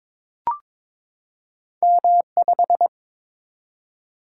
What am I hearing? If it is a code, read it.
Morse code, heard as a keyed tone, about 22 wpm, 700 Hz. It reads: M5
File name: practice / 22wpm / Two Letter-Number Combinations - 22wpm